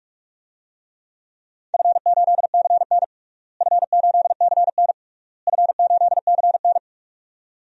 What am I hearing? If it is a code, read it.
Morse code, heard as a keyed tone, about 46 wpm, 690 Hz. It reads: F8CN F8CN F8CN
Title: DXCC-100 Call Signs - 4 Characters - Repeated 3x - 45wpm